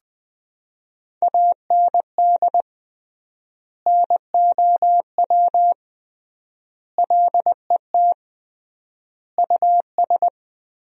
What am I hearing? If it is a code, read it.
Morse code, heard as a keyed tone, about 20 wpm, 700 Hz. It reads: AND NOW LET US
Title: 3x Word-Spacing - Sentences from Top 300 Words - 20wpm